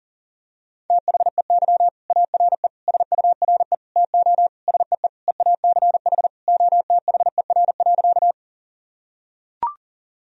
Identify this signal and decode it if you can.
Morse code, heard as a keyed tone, about 40 wpm, 700 Hz. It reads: THEY ARE SURE TO SEE EACH OTHER.
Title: Sentences from Top 1000 Words - Review Part 1 - 40wpm